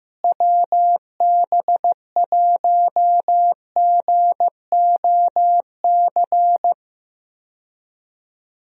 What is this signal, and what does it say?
Morse code, heard as a keyed tone, about 15 wpm, 700 Hz. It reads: WB1GOC
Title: USA Call Signs - 6 Characters - Repeated 3x - 15wpm